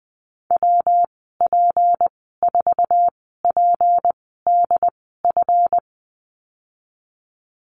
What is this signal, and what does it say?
Morse code, heard as a keyed tone, about 20 wpm, spaced as for 15 wpm, 700 Hz. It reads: WP4PDF
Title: 167 Lesson 27g - Number 7 - Call Signs 20wpm